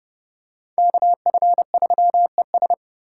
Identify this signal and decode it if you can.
Morse code, heard as a keyed tone, about 29 wpm, 700 Hz. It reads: KF3ES